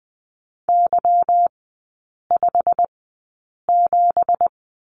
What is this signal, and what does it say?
Morse code, heard as a keyed tone, about 20 wpm, spaced as for 8 wpm, 700 Hz. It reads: Y57
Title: 166 Lesson 27e - Number 7 - Three Characters 20wpm